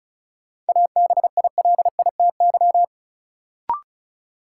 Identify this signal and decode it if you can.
Morse code, heard as a keyed tone, about 36 wpm, 700 Hz. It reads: ABILITY
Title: Sentences from Top 1200 Words - New Word Drill Alt - 35wpm